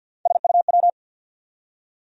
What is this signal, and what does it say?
Morse code, heard as a keyed tone, about 53 wpm, 700 Hz. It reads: SUW
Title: USA Call Sign Postfixes - 3 Letters 50wpm